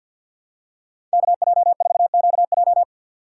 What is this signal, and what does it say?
Morse code, heard as a keyed tone, about 53 wpm, 700 Hz. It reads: KJ4QJ
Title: USA Call Signs - 5 Characters - 8x Character Spacing - 50wpm